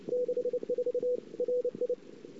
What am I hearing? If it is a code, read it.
Morse code, heard as a keyed tone, about 29 wpm, 490 Hz. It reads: 64 RI
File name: CW / 2008-ss-cw-k5tr-op-n5rz -22:00